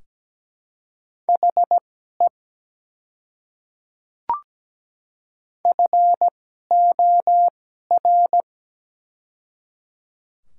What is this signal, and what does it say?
Morse code, heard as a keyed tone, about 17 wpm, spaced as for 12 wpm, 700 Hz. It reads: HE FOR
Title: 2x Character Spacing - Top 100 Words - 17wpm